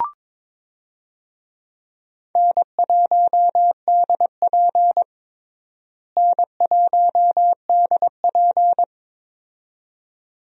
Morse code, heard as a keyed tone, about 22 wpm, 700 Hz. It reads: N1DP N1DP
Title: USA Call Signs - 4 Characters - Repeated 2x - 22wpm